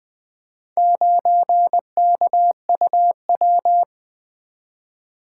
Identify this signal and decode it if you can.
Morse code, heard as a keyed tone, about 20 wpm, 700 Hz. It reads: 9KUW
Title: Four Letter-Number Combinations - 20wpm